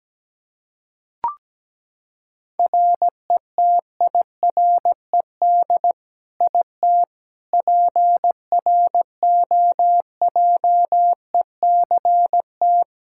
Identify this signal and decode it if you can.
Morse code, heard as a keyed tone, about 17 wpm, 700 Hz. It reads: RETIRED IT PROJECT
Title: Practice QSO - Exchange 3 - More Info - 17wpm